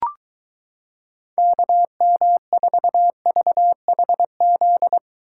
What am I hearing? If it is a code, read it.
Morse code, heard as a keyed tone, about 23 wpm, 700 Hz. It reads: KM4VHZ